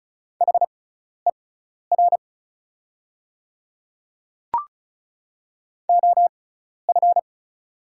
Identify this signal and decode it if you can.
Morse code, heard as a keyed tone, about 36 wpm, spaced as for 12 wpm, 700 Hz. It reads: HER OF
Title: 6x Character Spacing - Top 100 Words - 35wpm